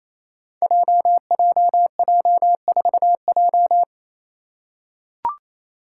Morse code, heard as a keyed tone, about 27 wpm, 700 Hz. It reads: JJJ4J
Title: Five Challenging Letter-Number Combinations 28wpm